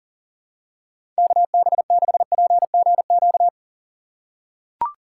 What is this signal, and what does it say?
Morse code, heard as a keyed tone, about 40 wpm, 690 Hz. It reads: KB6PGQ
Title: USA Call Signs - 4x Character Spacing - 40wpm